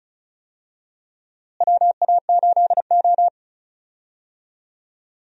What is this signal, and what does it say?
Morse code, heard as a keyed tone, about 36 wpm, 700 Hz. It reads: WA8O
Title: USA Call Signs - 4 Characters - Repeated 3x - 35wpm